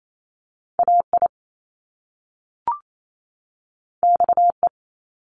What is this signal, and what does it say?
Morse code, heard as a keyed tone, about 28 wpm, 700 Hz. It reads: AI XE
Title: Top Two Letter Combinations in Words 28wpm